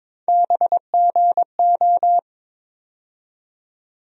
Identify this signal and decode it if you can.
Morse code, heard as a keyed tone, about 22 wpm, 700 Hz. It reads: BGO
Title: USA Call Sign Postfixes - 3 Letters 22wpm